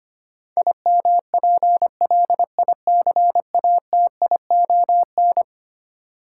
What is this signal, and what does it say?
Morse code, heard as a keyed tone, about 25 wpm, 700 Hz. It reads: IMPLICATION
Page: morse-code-ninja-practice-25wpm